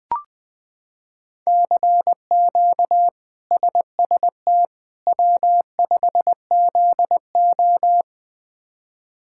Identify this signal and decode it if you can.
Morse code, heard as a keyed tone, about 19 wpm, 700 Hz. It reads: CQ SST W5ZO
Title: K1USN SST QSO Practice - 20wpm